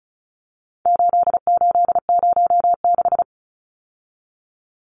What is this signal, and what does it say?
Morse code, heard as a keyed tone, about 34 wpm, 700 Hz. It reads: 8806
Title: Numbers - 4-Digit 35wpm